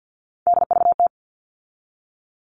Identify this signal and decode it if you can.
Morse code, heard as a keyed tone, about 50 wpm, 700 Hz. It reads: DVT
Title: All Cut Numbers - 3-Digit 50wpm